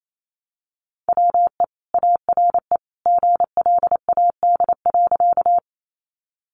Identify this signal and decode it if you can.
Morse code, heard as a keyed tone, about 28 wpm, 700 Hz. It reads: WE ARE GLAD.
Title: Sentences from Top 800 Words 28wpm